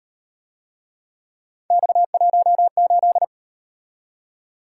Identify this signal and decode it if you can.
Morse code, heard as a keyed tone, about 37 wpm, 700 Hz. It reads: X18